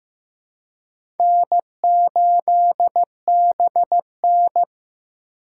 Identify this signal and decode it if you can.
Morse code, heard as a keyed tone, about 18 wpm, 700 Hz. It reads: N8BN
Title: Speed Racing - USA Call Signs 15wpm